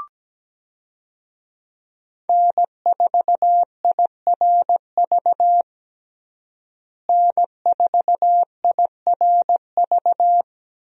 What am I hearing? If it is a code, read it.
Morse code, heard as a keyed tone, about 17 wpm, 700 Hz. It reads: N4IRV N4IRV
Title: USA Call Signs - 5 Characters - Repeated 2x - 17wpm